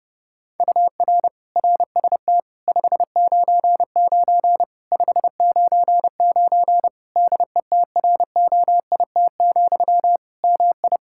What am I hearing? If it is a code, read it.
Morse code, heard as a keyed tone, about 32 wpm, 700 Hz. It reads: UR RST 599 599 DETROIT, MI
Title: Practice QSO - Exchange 1 - Beginning - 30wpm